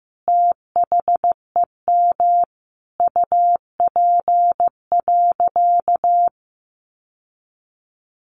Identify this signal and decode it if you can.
Morse code, heard as a keyed tone, about 15 wpm, 700 Hz. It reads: THEM UP.
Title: Sentences from Top 1500 Words 15wpm